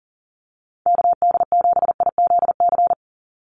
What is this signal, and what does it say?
Morse code, heard as a keyed tone, about 40 wpm, 700 Hz. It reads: KD7IZC
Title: USA Call Signs 40wpm